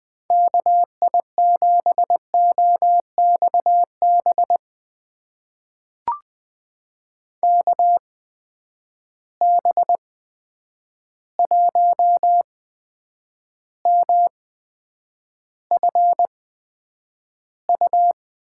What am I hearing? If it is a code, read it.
Morse code, heard as a keyed tone, about 21 wpm, spaced as for 9 wpm, 700 Hz. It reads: KI7OXB KB1MFU
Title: USA Call Signs - 6 Characters - 8x Character Spacing - 20wpm